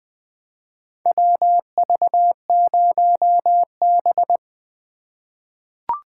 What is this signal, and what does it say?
Morse code, heard as a keyed tone, about 20 wpm, 700 Hz. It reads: WV0B
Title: USA Call Signs - 4 Characters - 3x Character Spacing - 20wpm